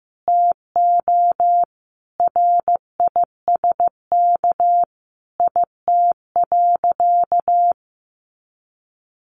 Morse code, heard as a keyed tone, about 15 wpm, 700 Hz. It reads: TO RISK IT.
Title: Sentences from Top 1500 Words 15wpm